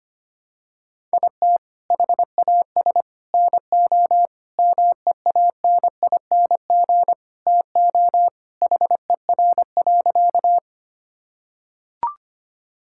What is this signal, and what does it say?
Morse code, heard as a keyed tone, about 25 wpm, 700 Hz. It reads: IT HAS NO MEANING TO HER.
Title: Sentences from Top 1000 Words - Review Part 6 - 25wpm